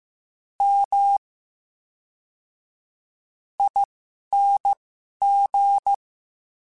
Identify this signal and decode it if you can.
Morse code, heard as a keyed tone, about 15 wpm, spaced as for 11 wpm, 800 Hz. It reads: M ING
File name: Special INGs / Special INGs-mp3s / VerySlowToFaster Ing @10